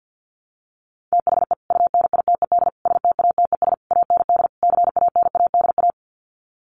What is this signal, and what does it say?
Morse code, heard as a keyed tone, about 50 wpm, 700 Hz. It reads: THE UNITED STATES AND CANADA
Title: Sets of 5 Words 50wpm